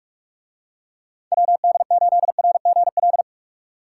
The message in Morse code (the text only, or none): WD8RGL